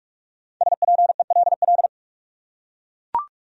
Morse code, heard as a keyed tone, about 44 wpm, 700 Hz. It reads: SPELL